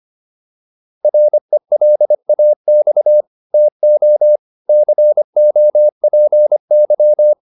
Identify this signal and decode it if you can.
Morse code, heard as a keyed tone, about 25 wpm, 600 Hz. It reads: RELAX TO COPY
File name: PR403 25